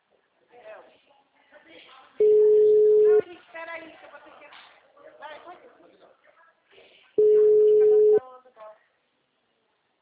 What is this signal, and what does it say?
Ringback tone (Europe, 425 Hz), with speech.
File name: out- -2006- - - .29168